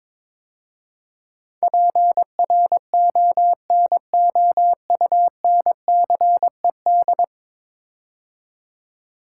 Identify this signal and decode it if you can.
Morse code, heard as a keyed tone, about 22 wpm, 700 Hz. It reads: PRONOUNCED
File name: practice / 22wpm / 10 Letter Words - First 3 Letters Given - 22wpm